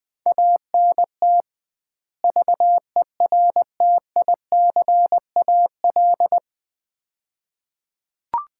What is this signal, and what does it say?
Morse code, heard as a keyed tone, about 20 wpm, 700 Hz. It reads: ANT VERTICAL
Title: Common QSO Phrases - 2x Word-Spacing - 20wpm